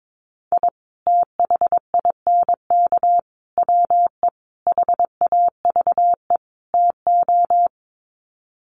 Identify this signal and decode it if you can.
Morse code, heard as a keyed tone, about 22 wpm, 700 Hz. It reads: I THINK WE HAVE TO
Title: Sets of 5 Words 22wpm